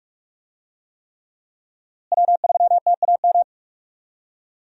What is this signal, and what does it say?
Morse code, heard as a keyed tone, about 46 wpm, 700 Hz. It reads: W3TAM
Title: USA Call Signs - 5 Characters - Repeated 3x - 45wpm